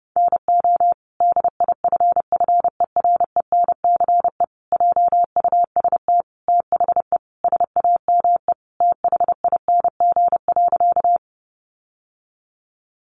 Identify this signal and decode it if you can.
Morse code, heard as a keyed tone, about 30 wpm, 700 Hz. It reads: NO DIFFERENCE JUST THE SAME THING.
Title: Sentences from Top 800 Words 30wpm